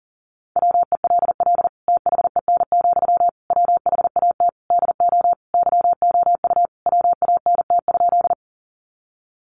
Morse code, heard as a keyed tone, about 40 wpm, 700 Hz. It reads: WELL THEN, WHAT DO YOU WANT?
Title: Sentences from Top 200 Words 40wpm